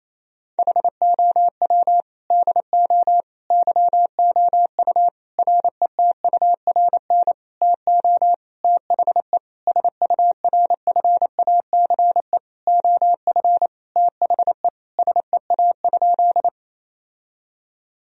Morse code, heard as a keyed tone, about 28 wpm, 700 Hz. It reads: HOW DO YOU RETURN TO THE SURFACE OF THE SEA?
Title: Sentences from Top 1000 Words - Review Part 3 - 28wpm